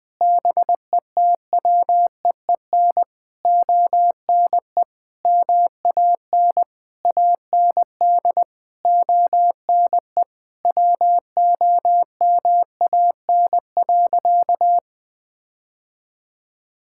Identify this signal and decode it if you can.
Morse code, heard as a keyed tone, about 21 wpm, 700 Hz. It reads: BETWEEN ONE MAN AND ONE WOMAN.
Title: Sentences with 6-Word Phrases - 20wpm